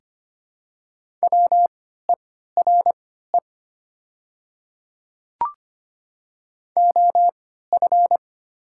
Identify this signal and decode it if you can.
Morse code, heard as a keyed tone, about 25 wpm, spaced as for 14 wpm, 700 Hz. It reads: WERE OF
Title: 3x Character Spacing - Top 200 Words - 25wpm